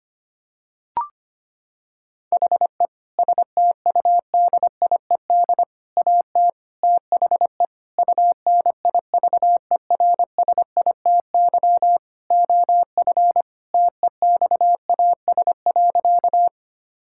Morse code, heard as a keyed tone, about 25 wpm, 700 Hz. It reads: HE STUDIED AT THE UNIVERSITY OF TEXAS.
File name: practice / 25wpm / Sentences with 4-Word Phrases - 25wpm